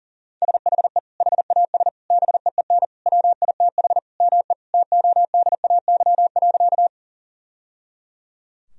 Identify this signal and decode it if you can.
Morse code, heard as a keyed tone, about 40 wpm, 700 Hz. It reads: SHE HAS BEEN WITH ME TODAY.
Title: Sentences from Top 1000 Words - Review Part 4 - 40wpm